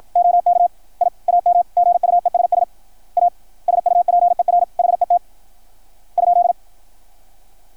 Morse code, heard as a keyed tone, about 53 wpm, 680 Hz. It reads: OK I AM GUESS A SUPER HET ?
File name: archived / CRSnet